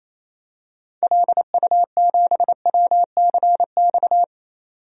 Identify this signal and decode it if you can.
Morse code, heard as a keyed tone, about 28 wpm, 700 Hz. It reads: LU7WCX
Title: DXCC-100 Call Signs - 6 Characters - 28wpm